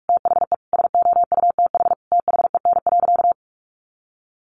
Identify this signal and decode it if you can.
Morse code, heard as a keyed tone, about 44 wpm, 700 Hz. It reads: THE SOUTH THEN.
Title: Sentences from Top 1400 Words 45wpm